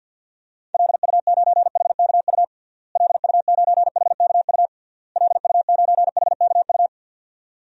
Morse code, heard as a keyed tone, about 46 wpm, 700 Hz. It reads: LU9HKU LU9HKU LU9HKU
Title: DXCC-100 Call Signs - 6 Characters - Repeated 3x - 50wpm